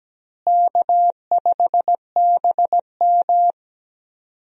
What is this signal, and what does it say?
Morse code, heard as a keyed tone, about 17 wpm, 700 Hz. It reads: K5BM
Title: USA Call Signs - 4 Characters - Repeated 3x - 17wpm